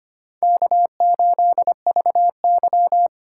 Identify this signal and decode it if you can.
Morse code, heard as a keyed tone, about 25 wpm, 700 Hz. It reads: K8VY
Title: USA Call Signs - 4 Characters - 4x Character Spacing - 25wpm